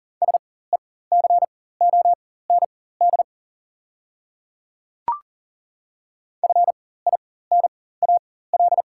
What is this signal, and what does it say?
Morse code, heard as a keyed tone, about 40 wpm, spaced as for 19 wpm, 690 Hz. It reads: SECOND FINAL
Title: 4x Character Spacing - Top 500 Words - 40wpm